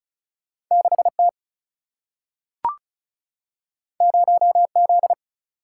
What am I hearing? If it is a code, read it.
Morse code, heard as a keyed tone, about 36 wpm, 700 Hz. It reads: 6 T 0 Z
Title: Single-Letter Single-Number 35wpm